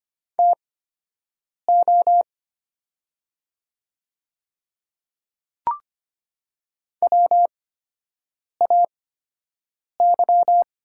Morse code, heard as a keyed tone, about 25 wpm, spaced as for 7 wpm, 700 Hz. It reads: TO WAY
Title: 8x Character Spacing - Top 100 Words - 25wpm